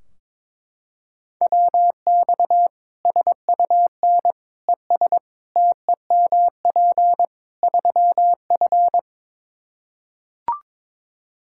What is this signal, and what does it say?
Morse code, heard as a keyed tone, about 21 wpm, 700 Hz. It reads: WX SUN ES TEMP 3F
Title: K1USN SST QSO Practice 22wpm